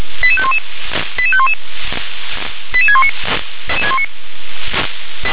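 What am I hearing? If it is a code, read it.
Five-tone selective call (ZVEI), heard as quick runs of five tones, tones of about 70 ms. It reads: 88511 88511 88619
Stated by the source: ZVEI five-tone selective call received on SDR.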